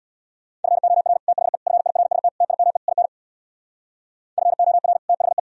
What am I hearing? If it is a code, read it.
Morse code, heard as a keyed tone, about 61 wpm, 700 Hz. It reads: FOR THE FIRST TIME IN FOR THE
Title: Speed-Racing - Sets of 5 Words 50wpm